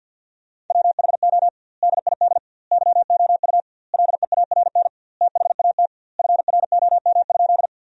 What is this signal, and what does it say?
Morse code, heard as a keyed tone, about 50 wpm, 700 Hz. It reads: WHO DID YOU LEARN THAT FROM?
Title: 2x Word-Spacing - Sentences from Top 300 Words - 50wpm